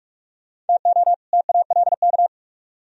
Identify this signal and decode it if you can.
Morse code, heard as a keyed tone, about 56 wpm, 700 Hz. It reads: TO TALK
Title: Speed-Racing - Sets of 2 Words 45wpm